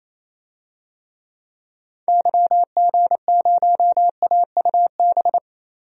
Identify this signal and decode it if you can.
Morse code, heard as a keyed tone, about 28 wpm, 700 Hz. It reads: YG0AUB
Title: DXCC-100 Call Signs - 6 Characters - 28wpm